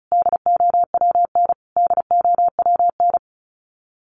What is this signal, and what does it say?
Morse code, heard as a keyed tone, about 34 wpm, 700 Hz. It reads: DOWN DOWN
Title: Top 500 Words Repeated 35wpm